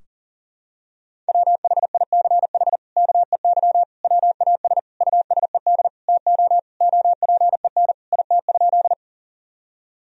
Morse code, heard as a keyed tone, about 40 wpm, 700 Hz. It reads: WHICH KEY WAS USED TO OPEN IT?
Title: Sentences from Top 1000 Words - Review Part 4 - 40wpm